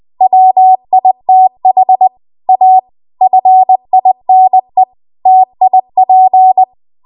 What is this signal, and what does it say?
Morse code, heard as a keyed tone, about 20 wpm, 750 Hz. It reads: WITH A FINE TIP